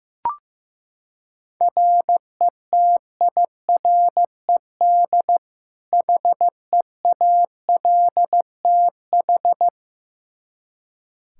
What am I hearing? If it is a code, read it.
Morse code, heard as a keyed tone, about 15 wpm, 700 Hz. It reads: RETIRED HEALTH
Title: Practice QSO - Exchange 3 - More Info - 15wpm